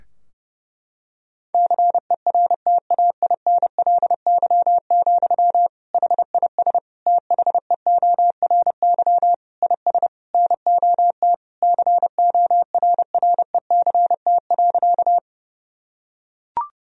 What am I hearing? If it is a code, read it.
Morse code, heard as a keyed tone, about 32 wpm, 700 Hz. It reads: CERTAINLY, HIS THEORY IS NOT CORRECT.
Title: Sentences from Top 1000 Words - Review Part 3 - 30wpm